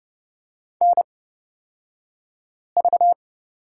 Morse code, heard as a keyed tone, about 31 wpm, spaced as for 5 wpm, 700 Hz. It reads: NV
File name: Lesson 36d - BT - Two Characters 30wpm